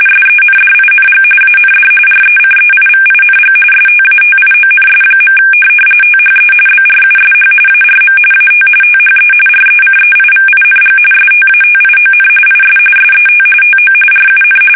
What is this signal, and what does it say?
RTTY, about 45.45 baud. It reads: SENDING AND RECEIVING STATIONS. THE BRUSH ARM AT THE RECEIVING STATION IS SET TO RUN
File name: fsk850